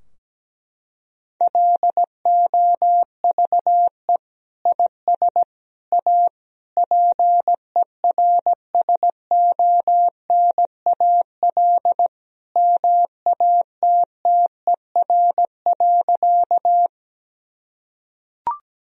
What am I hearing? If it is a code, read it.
Morse code, heard as a keyed tone, about 17 wpm, 700 Hz. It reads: LOVE IS A PERSONAL MATTER.
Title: Sentences from Top 1000 Words - Review Part 4 - 17wpm